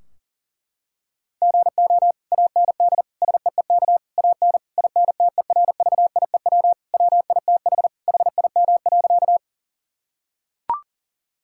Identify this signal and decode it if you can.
Morse code, heard as a keyed tone, about 40 wpm, 700 Hz. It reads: GO AND SEEK AN INTERVIEW WITH HIM.
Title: Sentences from Top 1600 Words 40wpm